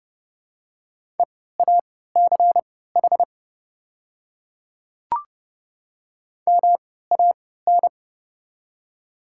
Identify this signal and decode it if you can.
Morse code, heard as a keyed tone, about 30 wpm, spaced as for 17 wpm, 700 Hz. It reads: EACH MAN
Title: 3x Character Spacing - Top 200 Words - 30wpm